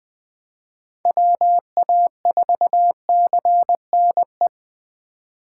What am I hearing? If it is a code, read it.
Morse code, heard as a keyed tone, about 20 wpm, 700 Hz. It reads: WA4CNE
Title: USA Call Signs - 4x Character Spacing - 20wpm